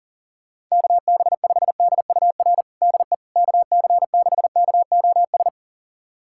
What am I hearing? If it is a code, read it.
Morse code, heard as a keyed tone, about 37 wpm, 700 Hz. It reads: KB5DUR DE KC6KOS
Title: USA Callsign DE Callsign - 6 Characters - 40wpm